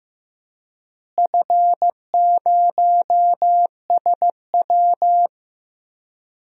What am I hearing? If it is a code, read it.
Morse code, heard as a keyed tone, about 15 wpm, 700 Hz. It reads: F0SW
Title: DXCC-100 Call Signs - 4 Characters - Repeated 3x - 15wpm